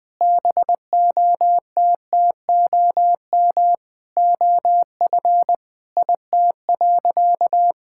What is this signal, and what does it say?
Morse code, heard as a keyed tone, about 20 wpm, 700 Hz. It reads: BOTTOM OF IT.
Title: Sentences from Top 1000 Words - Review Part 6 - 20wpm